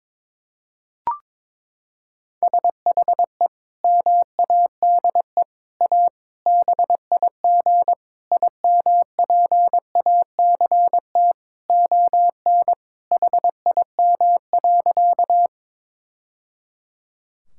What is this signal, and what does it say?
Morse code, heard as a keyed tone, about 22 wpm, 700 Hz. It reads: SHE MADE A BIG IMPACT ON HIM.
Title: Sentences from Top 1000 Words - Review Part 6 - 22wpm